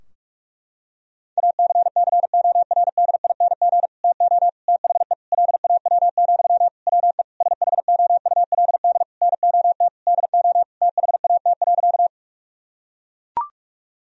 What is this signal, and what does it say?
Morse code, heard as a keyed tone, about 41 wpm, 700 Hz. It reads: ACCORDING TO THE LAW, WE SHOULD NOT DO THAT.
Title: Sentences from Top 1000 Words - Review Part 4 - 45wpm